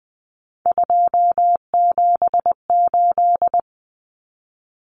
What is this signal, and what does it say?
Morse code, heard as a keyed tone, about 20 wpm, 700 Hz. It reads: 278
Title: Common Cut Numbers - 3-Digit 20wpm